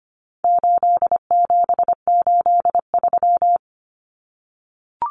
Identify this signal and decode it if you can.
Morse code, heard as a keyed tone, about 25 wpm, 700 Hz. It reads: 8783
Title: Common Cut Numbers - 4-Digit 25wpm